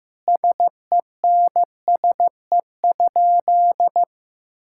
Morse code, heard as a keyed tone, about 15 wpm, 700 Hz. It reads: SENSE?
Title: Sentences from Top 1000 Words - Review Part 5 - 15wpm